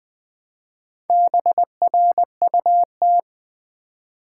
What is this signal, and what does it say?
Morse code, heard as a keyed tone, about 20 wpm, 700 Hz. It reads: BRUT
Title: Build A Word - 2 Letters At A Time - 6-Letter Words - 20wpm